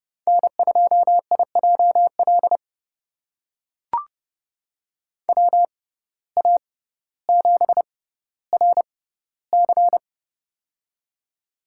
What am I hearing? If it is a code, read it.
Morse code, heard as a keyed tone, about 30 wpm, spaced as for 15 wpm, 700 Hz. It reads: N2IJL WA7RC
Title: USA Call Signs - 5 Characters - 6x Character Spacing - 30wpm